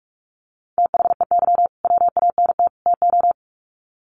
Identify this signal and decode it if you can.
Morse code, heard as a keyed tone, about 44 wpm, 700 Hz. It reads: THEY WANT TO
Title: Sets of 3 Words 45wpm